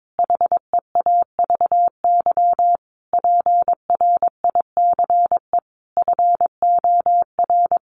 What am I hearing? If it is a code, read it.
Morse code, heard as a keyed tone, about 22 wpm, 700 Hz. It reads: HEAVY PRICE FOR
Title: Sentences from Top 1100 Words 22wpm